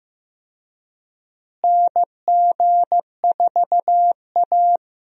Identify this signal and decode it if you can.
Morse code, heard as a keyed tone, about 15 wpm, 700 Hz. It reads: NG4A
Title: USA Call Signs - 4 Characters - Repeated 3x - 15wpm